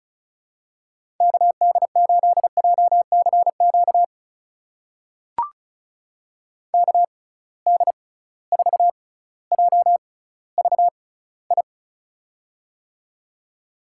Morse code, heard as a keyed tone, about 36 wpm, spaced as for 18 wpm, 700 Hz. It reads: KD8JCQ KD4JVI
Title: USA Call Signs - 6 Characters - 6x Character Spacing - 35wpm